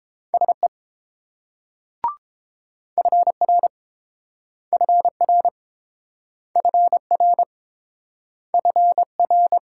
Morse code, heard as a keyed tone, about 30 wpm, 700 Hz. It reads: SE FR FR FR FR